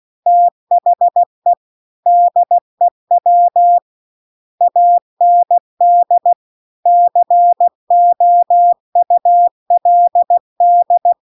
Morse code, heard as a keyed tone, about 16 wpm, 700 Hz. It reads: THE DEW AND COULD
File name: CW-The-Big-Twoheaded-River-2-16WPM